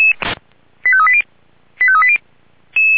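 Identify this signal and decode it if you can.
Five-tone selective call (ZVEI), heard as quick runs of five tones, tones of about 70 ms. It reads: 85380 85380
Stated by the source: ZVEI five-tone selective call 85703 received by SDR.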